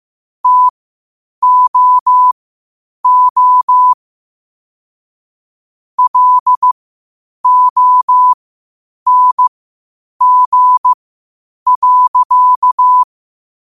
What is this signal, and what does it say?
Morse code, heard as a keyed tone, about 15 wpm, spaced as for 8 wpm, 1000 Hz. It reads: TOO LONG.
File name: qotdfiles / 8wpm